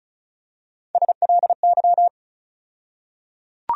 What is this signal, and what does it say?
Morse code, heard as a keyed tone, about 34 wpm, 700 Hz. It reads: SLY
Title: USA Call Sign Postfixes - 3 Letters 35wpm